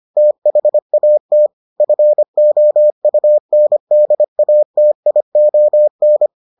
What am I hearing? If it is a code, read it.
Morse code, heard as a keyed tone, about 25 wpm, 600 Hz. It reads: THAT FOUNDATION